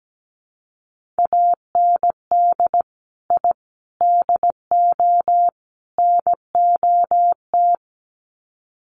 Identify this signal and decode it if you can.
Morse code, heard as a keyed tone, about 17 wpm, 700 Hz. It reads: AND I DO NOT
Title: Sets of 4 Words 17wpm